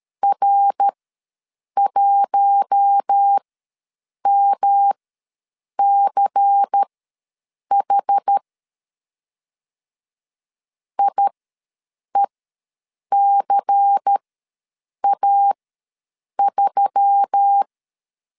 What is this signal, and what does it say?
Morse code, heard as a keyed tone, about 13 wpm, spaced as for 7 wpm, 800 Hz. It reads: R1MCH IECA3
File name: LEZIONI / 5 LBVC / 5 LBVC ws 12wpm 3159